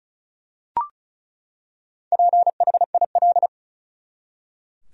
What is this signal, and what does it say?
Morse code, heard as a keyed tone, about 34 wpm, 700 Hz. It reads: PHIL
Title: USA Common Names - Max 5 Letters - 35wpm